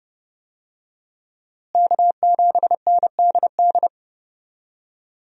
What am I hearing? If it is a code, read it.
Morse code, heard as a keyed tone, about 32 wpm, 700 Hz. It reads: K7NDD
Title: USA Call Signs - 5 Characters - Repeated 3x - 30wpm